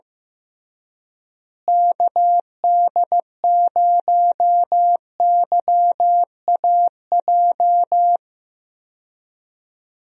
Morse code, heard as a keyed tone, about 15 wpm, 700 Hz. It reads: KD0YAJ
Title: USA Call Signs - 6 Characters - Repeated 3x - 15wpm